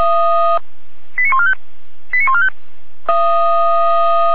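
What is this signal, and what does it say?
Five-tone selective call (ZVEI), heard as quick runs of five tones, tones of about 70 ms. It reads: 89145 89145
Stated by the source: ZVEI five-tone selective call received with an SDR for Feuerwehr Orlamünde, a fire brigade.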